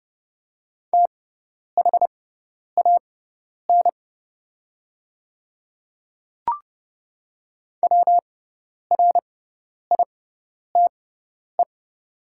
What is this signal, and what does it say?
Morse code, heard as a keyed tone, about 32 wpm, spaced as for 10 wpm, 700 Hz. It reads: THAN WRITE
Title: 6x Character Spacing - Top 100 Words - 30wpm